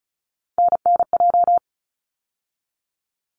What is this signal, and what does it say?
Morse code, heard as a keyed tone, about 36 wpm, 700 Hz. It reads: NNJ
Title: ARRL Sections 35wpm